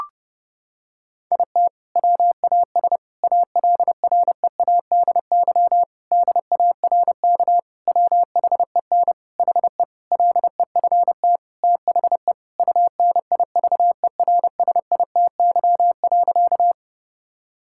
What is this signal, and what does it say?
Morse code, heard as a keyed tone, about 32 wpm, 700 Hz. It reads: IT WAS ALREADY DARK WHEN HE LEFT THE UNIVERSITY.
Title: Sentences from Top 1000 Words - Review Part 5 - 30wpm